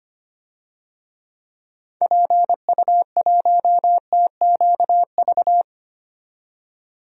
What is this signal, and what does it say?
Morse code, heard as a keyed tone, about 25 wpm, 700 Hz. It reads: PU1TQV
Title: DXCC-100 Call Signs - 6 Characters - 25wpm